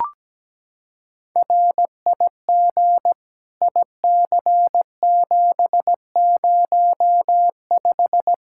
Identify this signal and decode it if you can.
Morse code, heard as a keyed tone, about 17 wpm, 700 Hz. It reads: RIG IC705
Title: Practice QSO - Exchange 2 - Info - 17wpm